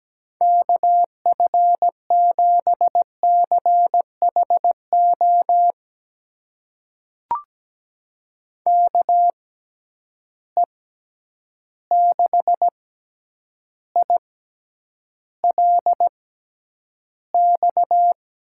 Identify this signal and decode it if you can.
Morse code, heard as a keyed tone, about 17 wpm, spaced as for 9 wpm, 700 Hz. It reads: KF7CHO KE6ILX
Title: USA Call Signs - 6 Characters - 6x Character Spacing - 17wpm